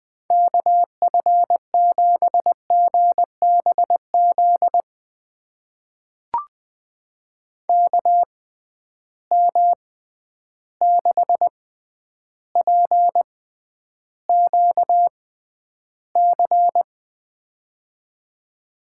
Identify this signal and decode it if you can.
Morse code, heard as a keyed tone, about 20 wpm, spaced as for 10 wpm, 700 Hz. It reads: KF7GBZ KM6PQC
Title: USA Call Signs - 6 Characters - 6x Character Spacing - 20wpm